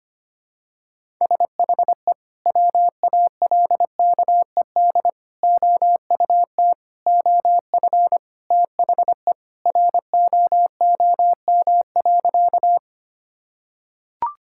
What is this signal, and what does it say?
Morse code, heard as a keyed tone, about 25 wpm, 700 Hz. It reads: SHE WALKED OUT OF THE ROOM.
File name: practice / 25wpm / Sentences with 3-Word Phrases - 25wpm